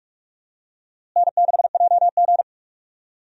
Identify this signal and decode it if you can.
Morse code, heard as a keyed tone, about 46 wpm, 700 Hz. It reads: N6JG
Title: USA Call Signs - 4 Characters - Repeated 2x - 45wpm